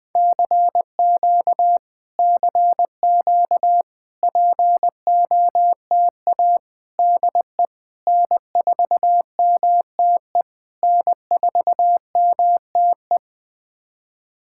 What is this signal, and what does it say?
Morse code, heard as a keyed tone, about 20 wpm, 700 Hz. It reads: CQ CQ POTA DE N4MTE N4MTE
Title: POTA QSO Practice - 20wpm